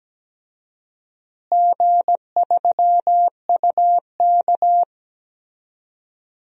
Morse code, heard as a keyed tone, about 17 wpm, 700 Hz. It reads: G3UK
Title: DXCC-100 Call Signs - 4 Characters - Repeated 3x - 17wpm